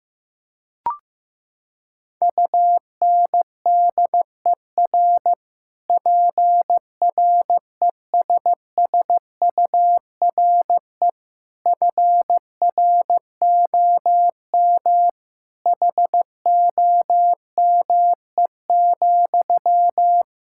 Morse code, heard as a keyed tone, about 15 wpm, 700 Hz. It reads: UNDER PRESSURE FROM HOME,
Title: Sentences from Top 1000 Words - Review Part 6 - 15wpm